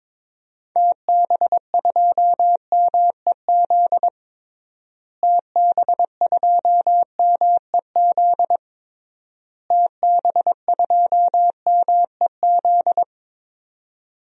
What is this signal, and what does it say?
Morse code, heard as a keyed tone, about 22 wpm, 700 Hz. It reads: TB2MEZ TB2MEZ TB2MEZ
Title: DXCC-100 Call Signs - 6 Characters - Repeated 3x - 22wpm